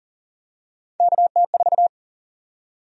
Morse code, heard as a keyed tone, about 40 wpm, 700 Hz. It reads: KT4
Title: USA Call Sign Prefixes - Repeated 2x - 40wpm